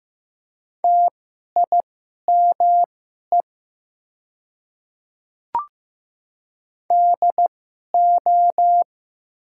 Morse code, heard as a keyed tone, about 15 wpm, spaced as for 11 wpm, 700 Hz. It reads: TIME DO
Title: 2x Character Spacing - Top 100 Words - 15wpm